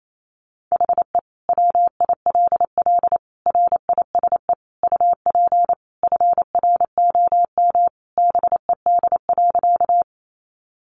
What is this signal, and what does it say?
Morse code, heard as a keyed tone, about 28 wpm, 700 Hz. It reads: HE WILL RISE UP FROM BED.
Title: Sentences from Top 900 Words 28wpm